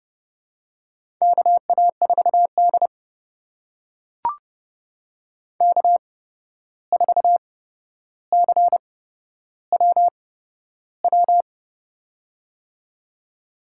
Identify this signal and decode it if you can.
Morse code, heard as a keyed tone, about 30 wpm, spaced as for 12 wpm, 700 Hz. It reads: KA4D K4CWW
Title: USA Call Signs - 8x Character Spacing - 30wpm